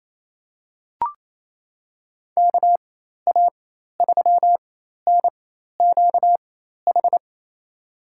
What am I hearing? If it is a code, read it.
Morse code, heard as a keyed tone, about 28 wpm, spaced as for 13 wpm, 700 Hz. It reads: KA3NQH
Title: USA Call Signs - 6 Characters - 4x Character Spacing - 28wpm